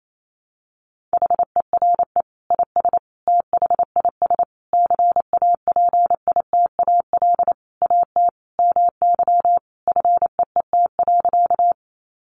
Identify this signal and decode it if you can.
Morse code, heard as a keyed tone, about 28 wpm, 700 Hz. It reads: HERE IS THIS CAPITAL AT MY FEET.
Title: Sentences from Top 1000 Words 28wpm